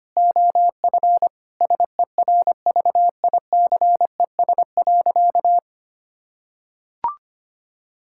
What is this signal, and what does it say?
Morse code, heard as a keyed tone, about 25 wpm, 700 Hz. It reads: OF SERVICES.
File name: practice / 25wpm / Sentences with 4-Word Phrases - 25wpm